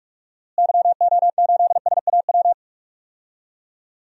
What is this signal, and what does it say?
Morse code, heard as a keyed tone, about 43 wpm, 700 Hz. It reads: YO8SAW
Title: DXCC-100 Call Signs - 6 Characters - Repeated 3x - 45wpm